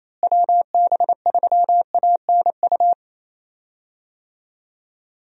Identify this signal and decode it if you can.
Morse code, heard as a keyed tone, about 28 wpm, 700 Hz. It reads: WB3ANU
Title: USA Call Signs - 6 Characters - Repeated 2x - 28wpm